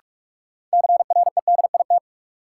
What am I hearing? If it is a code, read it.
Morse code, heard as a keyed tone, about 44 wpm, 700 Hz. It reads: CREDIT
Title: Sentences from Top 1000 Words - New Word Drill - 45wpm